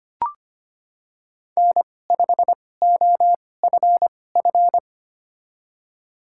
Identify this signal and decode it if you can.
Morse code, heard as a keyed tone, about 25 wpm, spaced as for 18 wpm, 700 Hz. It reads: N5OFF
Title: USA Call Signs - 5 Characters - 2x Character Spacing - 25wpm